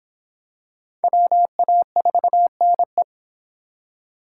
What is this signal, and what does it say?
Morse code, heard as a keyed tone, about 26 wpm, 700 Hz. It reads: WA4NE